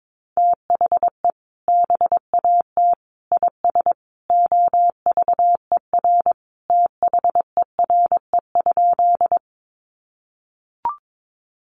Morse code, heard as a keyed tone, about 22 wpm, 700 Hz. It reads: THE BAT IS OVER THERE?
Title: Sentences from Top 1500 Words 22wpm